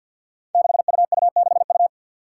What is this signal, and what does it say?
Morse code, heard as a keyed tone, about 48 wpm, 700 Hz. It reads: 6UU6U
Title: Five Challenging Letter-Number Combinations 50wpm